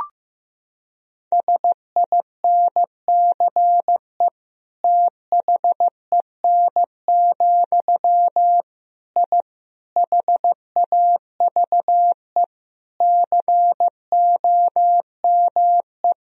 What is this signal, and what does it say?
Morse code, heard as a keyed tone, about 15 wpm, 700 Hz. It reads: SINCE THEN, I HAVE COME
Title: Sentences from Top 1000 Words - Review Part 1 - 15wpm